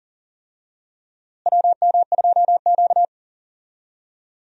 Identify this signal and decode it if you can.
Morse code, heard as a keyed tone, about 40 wpm, 700 Hz. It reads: WM2Q
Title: USA Call Signs - 4 Characters - Repeated 2x - 40wpm